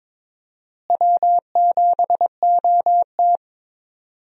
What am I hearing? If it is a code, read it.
Morse code, heard as a keyed tone, about 22 wpm, 700 Hz. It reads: W7OT
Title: USA Call Signs - 4 Characters - Repeated 2x - 22wpm